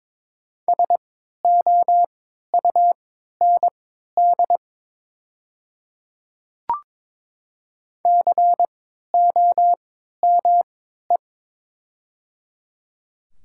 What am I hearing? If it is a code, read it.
Morse code, heard as a keyed tone, about 22 wpm, spaced as for 12 wpm, 700 Hz. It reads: SOUND COME
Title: 3x Character Spacing - Top 100 Words - 22wpm